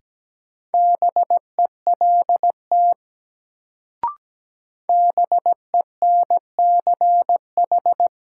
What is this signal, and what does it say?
Morse code, heard as a keyed tone, about 17 wpm, 700 Hz. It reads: BELT BENCH
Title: Sentences from Top 1400 Words - New Word Drill - 17wpm